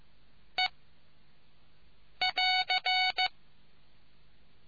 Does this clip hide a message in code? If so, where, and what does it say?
Yes: Morse code from 0.6 to 3.3 s, heard as a keyed tone, about 15 wpm, 740 Hz. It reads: E <AR>